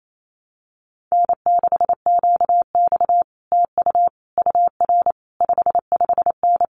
Morse code, heard as a keyed tone, about 28 wpm, 700 Hz. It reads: N6QX TU UR 55N
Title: Common QSO Phrases Encore 28wpm